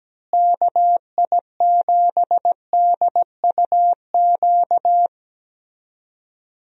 Morse code, heard as a keyed tone, about 17 wpm, 700 Hz. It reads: KI7DUQ
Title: USA Call Signs - 6 Characters - Repeated 3x - 17wpm